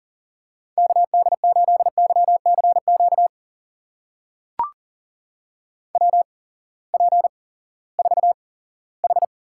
Morse code, heard as a keyed tone, about 40 wpm, spaced as for 20 wpm, 700 Hz. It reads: KD8YCQ WP4H
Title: USA Call Signs - 6 Characters - 8x Character Spacing - 40wpm